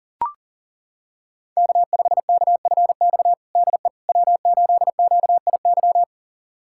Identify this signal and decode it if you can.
Morse code, heard as a keyed tone, about 40 wpm, 700 Hz. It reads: K5KFX DE W8QIY
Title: USA Callsign DE Callsign - 5 Characters - 40wpm